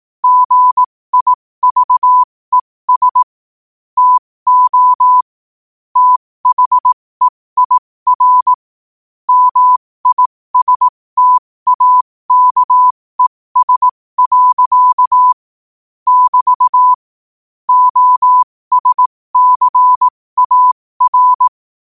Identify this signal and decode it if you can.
Morse code, heard as a keyed tone, about 18 wpm, spaced as for 15 wpm, 1000 Hz. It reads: GIVES TO THEIR MISTAKES. <BT> OSCAR
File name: qotdfiles / 15wpm